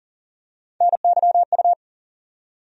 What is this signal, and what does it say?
Morse code, heard as a keyed tone, about 40 wpm, 700 Hz. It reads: NYU
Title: USA Call Sign Postfixes - 3 Letters 40wpm